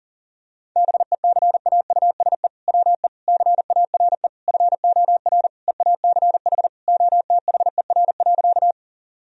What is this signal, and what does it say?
Morse code, heard as a keyed tone, about 37 wpm, 700 Hz. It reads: BECAUSE WE CARE FOR EACH OTHER.
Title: Sentences from Top 1000 Words - Review Part 3 - 40wpm